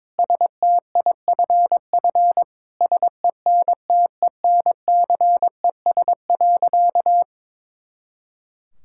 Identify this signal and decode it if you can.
Morse code, heard as a keyed tone, about 22 wpm, 700 Hz. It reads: STIFF SENTENCES.
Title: Pangram Challenge v2 - 22wpm